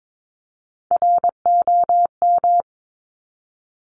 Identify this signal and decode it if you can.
Morse code, heard as a keyed tone, about 22 wpm, 700 Hz. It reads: ROM
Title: Top Three Letter Combinations in Words 22wpm